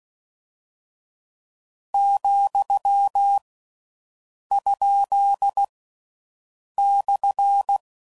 Morse code, heard as a keyed tone, about 16 wpm, 800 Hz. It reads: , ? /